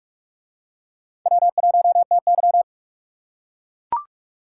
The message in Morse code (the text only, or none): W1TY